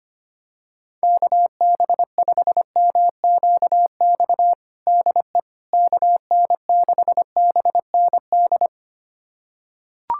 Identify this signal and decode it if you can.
Morse code, heard as a keyed tone, about 25 wpm, 700 Hz. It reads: KB5MQX DE KN6BND
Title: USA Callsign DE Callsign 25wpm